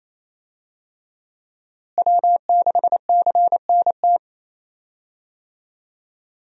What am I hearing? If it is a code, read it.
Morse code, heard as a keyed tone, about 28 wpm, 700 Hz. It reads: W6CNT
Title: USA Call Signs - 5 Characters - Repeated 2x - 28wpm